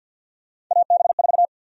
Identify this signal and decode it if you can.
Morse code, heard as a keyed tone, about 47 wpm, 700 Hz. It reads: AB4
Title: USA Call Sign Prefixes - Repeated 2x - 50wpm